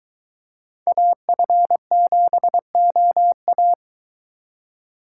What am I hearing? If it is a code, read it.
Morse code, heard as a keyed tone, about 22 wpm, 700 Hz. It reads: AF7OA